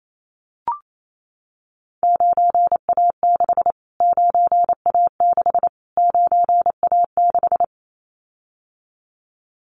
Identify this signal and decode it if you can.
Morse code, heard as a keyed tone, about 28 wpm, 700 Hz. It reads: 9A6 9A6 9A6
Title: Callsign Prefixes for Top 25 DXCC - Repeated 3x - 28wpm